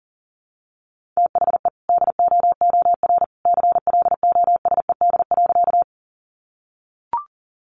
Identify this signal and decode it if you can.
Morse code, heard as a keyed tone, about 37 wpm, 700 Hz. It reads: THE DOOR CLOSED.
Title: Sentences from Top 1400 Words 40wpm